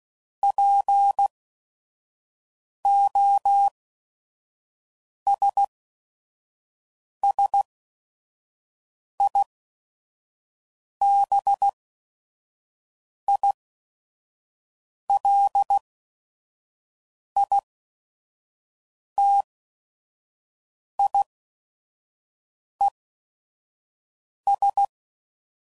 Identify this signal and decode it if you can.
Morse code, heard as a keyed tone, about 16 wpm, spaced as for 5 wpm, 800 Hz. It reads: POSSIBILITIES
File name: CW Code Groups @5WPM / Technical Subjects / OR Gate